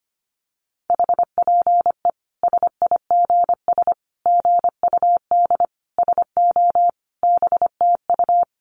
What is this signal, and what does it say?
Morse code, heard as a keyed tone, about 25 wpm, 700 Hz. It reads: HPE SIGS GUD SO BTU
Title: Common QSO Phrases 25wpm